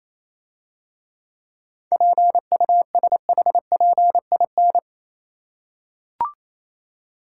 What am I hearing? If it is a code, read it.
Morse code, heard as a keyed tone, about 26 wpm, 690 Hz. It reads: PUSHPIN
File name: Easy Words - Speed-Racing with Hints - 22wpm